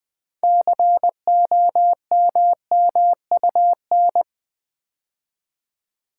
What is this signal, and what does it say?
Morse code, heard as a keyed tone, about 20 wpm, 700 Hz. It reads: COMMUN
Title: Build A Word - 3 Letters At A Time - 9-Letter Words - 20wpm